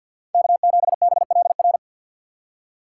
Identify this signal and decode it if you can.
Morse code, heard as a keyed tone, about 52 wpm, 700 Hz. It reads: K7BLR
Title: USA Call Signs - 5 Characters - 2x Character Spacing - 50wpm